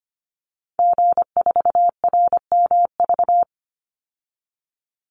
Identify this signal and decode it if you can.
Morse code, heard as a keyed tone, about 25 wpm, 700 Hz. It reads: G4RMV
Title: Call Signs Repeated 3x - 25wpm